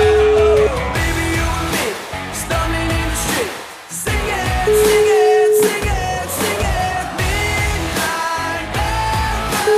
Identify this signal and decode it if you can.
Ringback tone (Europe, 425 Hz), with noise.